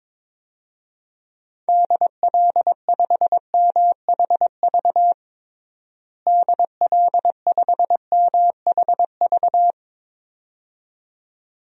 Morse code, heard as a keyed tone, about 22 wpm, 700 Hz. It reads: DL5MHV DL5MHV
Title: DXCC-100 Call Signs - 6 Characters - Repeated 2x - 22wpm